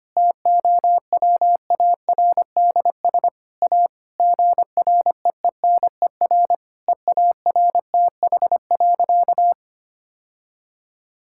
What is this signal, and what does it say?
Morse code, heard as a keyed tone, about 25 wpm, 700 Hz. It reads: TOWARDS A GREENER EARTH.
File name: practice / 25wpm / Difficult Sentences with Hints - 25wpm